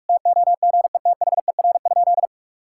Morse code, heard as a keyed tone, about 46 wpm, 700 Hz. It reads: TOGETHER?
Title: Sentences from Top 1600 Words 45wpm